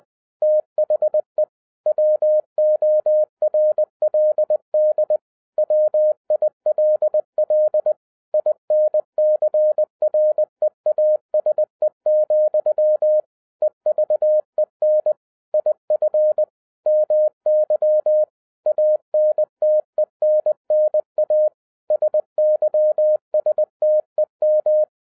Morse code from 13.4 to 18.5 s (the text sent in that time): EVEN IF MY